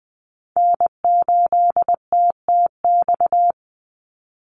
Morse code, heard as a keyed tone, about 20 wpm, 700 Hz. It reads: N8TTX
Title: USA Call Signs 20wpm